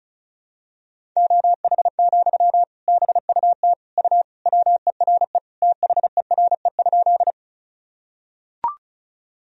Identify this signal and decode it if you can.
Morse code, heard as a keyed tone, about 34 wpm, 700 Hz. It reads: OH, BUT U WERE THERE?
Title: Sentences from Top 500 Words and Abbreviations 35wpm